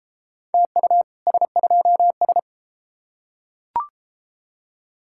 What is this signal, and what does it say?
Morse code, heard as a keyed tone, about 32 wpm, 700 Hz. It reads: TU S2S